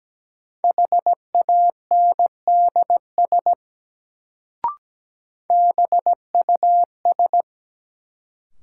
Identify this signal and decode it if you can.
Morse code, heard as a keyed tone, about 17 wpm, 700 Hz. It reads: HANDS BUS
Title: Sentences from Top 1000 Words - Review Part 6 - New Word Drill - 17wpm